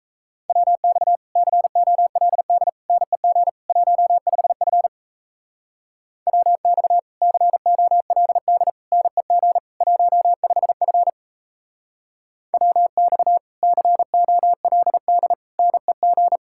Morse code, heard as a keyed tone, about 35 wpm, 700 Hz. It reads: WX COLD NEG 15F WX COLD NEG 15F WX COLD NEG